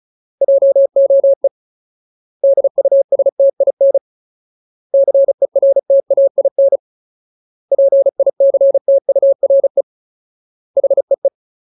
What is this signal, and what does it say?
Morse code, heard as a keyed tone, about 34 wpm, 550 Hz. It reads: JOE DUSTIN CERTAIN PICTURE HEE
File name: Ultimate Word List / 35WPM Ultimate